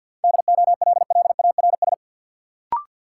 Morse code, heard as a keyed tone, about 51 wpm, 700 Hz. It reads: DOLLARS
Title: Sets of 5 Words - New Word Drill - 50wpm